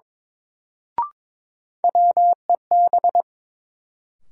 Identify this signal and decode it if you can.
Morse code, heard as a keyed tone, about 22 wpm, 700 Hz. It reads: WEB
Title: Sentences from Top 1000 Words - Review Part 4 - New Word Drill - 22wpm